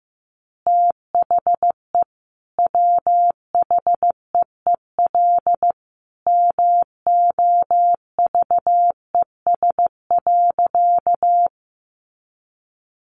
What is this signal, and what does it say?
Morse code, heard as a keyed tone, about 15 wpm, 700 Hz. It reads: THE WHEEL MOVES.
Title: Additional Easy Phrases 15wpm